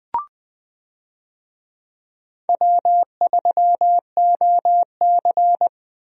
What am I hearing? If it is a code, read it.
Morse code, heard as a keyed tone, about 20 wpm, 700 Hz. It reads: W3OC
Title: USA Call Signs - 4 Characters - Repeated 2x - 20wpm